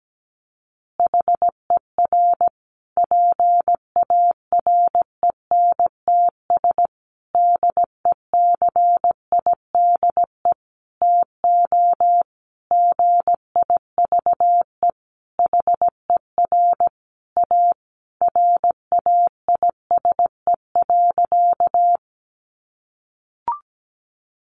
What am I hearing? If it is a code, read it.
Morse code, heard as a keyed tone, about 17 wpm, 700 Hz. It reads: HER PARENTS DECIDE TO GIVE HER A RAISE.
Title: Easy Phrases 17wpm